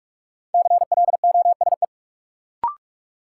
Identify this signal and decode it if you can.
Morse code, heard as a keyed tone, about 44 wpm, 700 Hz. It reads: CLOSE
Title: Sentences from Top 300 Words - New Word Drill - 45wpm